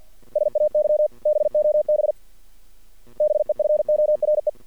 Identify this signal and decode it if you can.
Morse code, heard as a keyed tone, about 48 wpm, 630 Hz. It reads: SAY BOB BEFORE
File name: archived / CRSnet